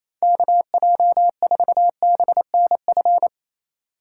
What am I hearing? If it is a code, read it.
Morse code, heard as a keyed tone, about 35 wpm, 700 Hz. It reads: KJ4BNF
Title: Speed Racing - USA 2x3 Call Signs 28wpm